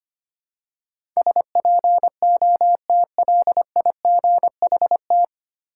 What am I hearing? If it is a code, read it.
Morse code, heard as a keyed tone, about 25 wpm, 700 Hz. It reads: SPOTLIGHT
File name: practice / 25wpm / 9 Letter Words - First 3 Letters Given - 25wpm